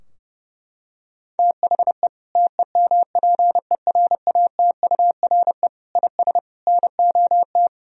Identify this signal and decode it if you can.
Morse code, heard as a keyed tone, about 32 wpm, 700 Hz. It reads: THE TEMPERATURE IS NOT
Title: Sentences from Top 1000 Words - Review Part 4 - 30wpm